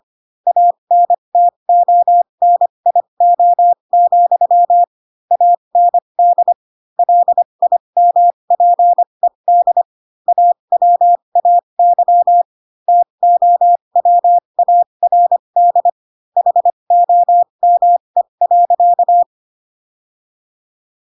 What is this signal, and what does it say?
Morse code, heard as a keyed tone, about 25 wpm, spaced as for 22 wpm, 700 Hz. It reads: ANTONIO, AND LIMPED AWAY TOWARD HOME.
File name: Pinocchio / Pinocchio ch2 18 25 ik1hge